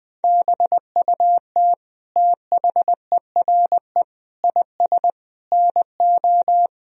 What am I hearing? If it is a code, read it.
Morse code, heard as a keyed tone, about 20 wpm, 700 Hz. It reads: BUT THERE IS NO
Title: Sentences from Top 1000 Words - Review Part 3 - 20wpm